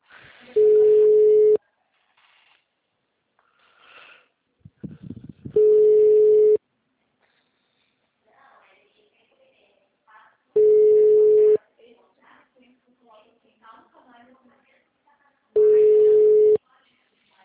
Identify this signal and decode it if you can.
Ringback tone (Europe, 425 Hz), with speech.
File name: out- -2024- - - .43